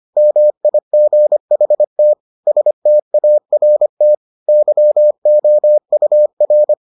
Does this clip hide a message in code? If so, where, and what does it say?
Yes: Morse code from 0.2 to 6.7 s, heard as a keyed tone, about 25 wpm, 600 Hz. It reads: MIGHT START YOUR